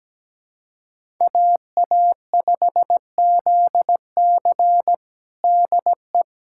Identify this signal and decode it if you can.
Morse code, heard as a keyed tone, about 17 wpm, 700 Hz. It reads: AA5ZC DE
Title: USA Callsign DE Callsign - 5 Characters - 17wpm